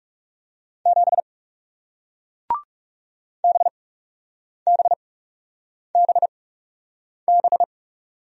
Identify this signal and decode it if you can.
Morse code, heard as a keyed tone, about 37 wpm, spaced as for 8 wpm, 700 Hz. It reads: 7 BBBB
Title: Speed-Racing Single Letter-Number 30wpm